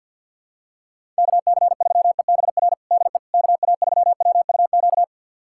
Morse code, heard as a keyed tone, about 50 wpm, 700 Hz. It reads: KC3EBR DE KA3WUQ
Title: USA Callsign DE Callsign - 6 Characters - 50wpm